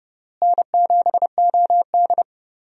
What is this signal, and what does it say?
Morse code, heard as a keyed tone, about 29 wpm, 700 Hz. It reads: N7OD
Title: USA Call Signs - 4 Characters - Repeated 3x - 30wpm